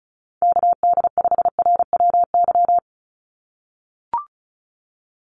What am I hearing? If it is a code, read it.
Morse code, heard as a keyed tone, about 36 wpm, 700 Hz. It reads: KD5RWY
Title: USA Call Signs 35wpm